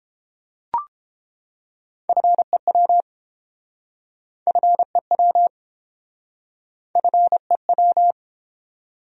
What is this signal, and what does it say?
Morse code, heard as a keyed tone, about 31 wpm, 700 Hz. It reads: FEW FEW FEW